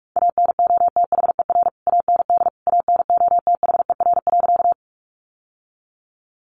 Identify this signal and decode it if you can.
Morse code, heard as a keyed tone, about 46 wpm, 700 Hz. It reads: ANOTHER AND ANOTHER.
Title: Sentences from Top 600 Words 45wpm